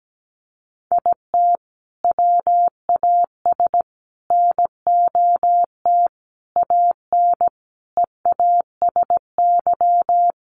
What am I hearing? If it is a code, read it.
Morse code, heard as a keyed tone, about 17 wpm, 700 Hz. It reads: IT WAS NOT AN EASY
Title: Sentences from Top 1300 Words 17wpm